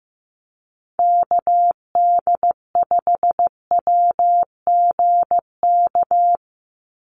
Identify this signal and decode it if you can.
Morse code, heard as a keyed tone, about 15 wpm, 700 Hz. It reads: KD5WGK
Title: USA Call Signs 15wpm